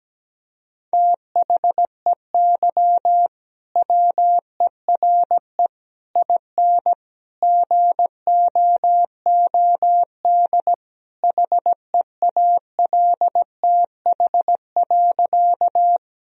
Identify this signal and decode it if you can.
Morse code, heard as a keyed tone, about 17 wpm, 700 Hz. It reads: THEY WERE IN GOOD HEALTH.
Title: Sentences from Top 1000 Words - Review Part 3 - 17wpm